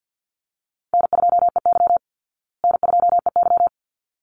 Morse code, heard as a keyed tone, about 48 wpm, 700 Hz. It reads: N2EY N2EY
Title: Call Signs Repeated 3x - 50wpm